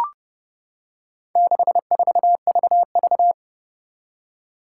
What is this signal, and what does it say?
Morse code, heard as a keyed tone, about 31 wpm, 700 Hz. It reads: 64VV
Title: Four Challenging Letter-Number Combinations 30wpm